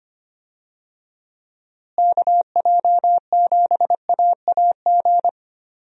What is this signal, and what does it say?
Morse code, heard as a keyed tone, about 25 wpm, 700 Hz. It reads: KJ7AAG
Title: USA Call Signs - 6 Characters - Repeated 2x - 25wpm